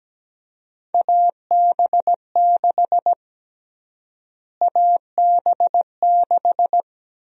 Morse code, heard as a keyed tone, about 17 wpm, 700 Hz. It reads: AB6 AB6
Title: USA Call Sign Prefixes - Repeated 2x - 17wpm